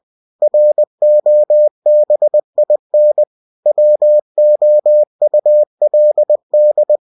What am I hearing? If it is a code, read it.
Morse code, heard as a keyed tone, about 20 wpm, 600 Hz. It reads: ROBIN WOULD
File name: cw-books / winnie-the-pooh / winnie-20wpm-page-0092